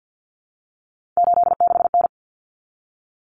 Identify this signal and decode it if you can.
Morse code, heard as a keyed tone, about 50 wpm, 700 Hz. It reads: 86N
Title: Common Cut Numbers - 3-Digit 50wpm